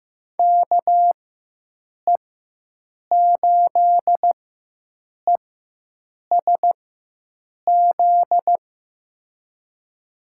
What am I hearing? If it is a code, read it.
Morse code, heard as a keyed tone, about 15 wpm, spaced as for 7 wpm, 700 Hz. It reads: KE8ESZ
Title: USA Call Signs - 4x Character Spacing - 15wpm